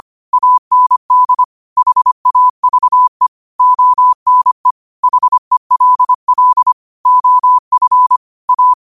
Morse code, heard as a keyed tone, about 25 wpm, 1000 Hz. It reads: AND HAVE ONE HELL OF A
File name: qotdfiles / 25wpm